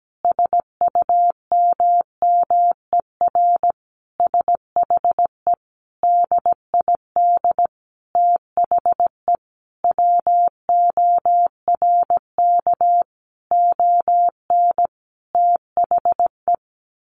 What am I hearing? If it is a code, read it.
Morse code, heard as a keyed tone, about 17 wpm, 700 Hz. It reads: SUMMER SHE DID THE WORK ON THE
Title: Sentences from Top 1200 Words 17wpm